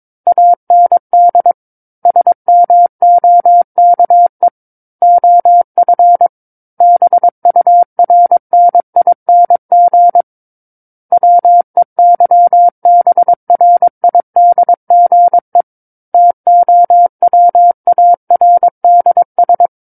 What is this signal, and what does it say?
Morse code, heard as a keyed tone, about 22 wpm, 700 Hz. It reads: AND SMOKE OF BURNING WEYBRIDGE TOWARDS
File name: Book One / 13-Chapter 13 - 22 WPM